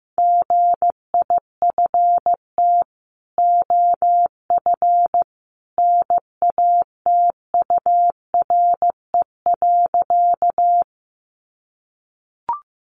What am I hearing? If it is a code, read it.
Morse code, heard as a keyed tone, about 15 wpm, 700 Hz. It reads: GIFT OF NATURE.
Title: Sentences from Top 900 Words 15wpm